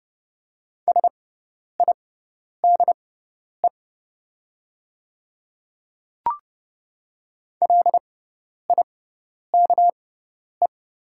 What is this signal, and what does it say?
Morse code, heard as a keyed tone, about 30 wpm, spaced as for 10 wpm, 700 Hz. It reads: SIDE LIKE
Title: 6x Character Spacing - Top 500 Words - 30wpm